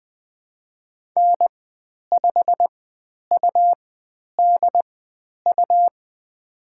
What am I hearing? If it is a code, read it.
Morse code, heard as a keyed tone, about 20 wpm, spaced as for 10 wpm, 700 Hz. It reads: N5UDU
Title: 102 Lesson 17g - Letter u - Call Signs 20wpm